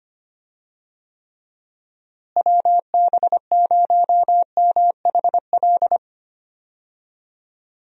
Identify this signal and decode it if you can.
Morse code, heard as a keyed tone, about 25 wpm, 700 Hz. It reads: WB0MHL
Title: USA Call Signs - 6 Characters - Repeated 2x - 25wpm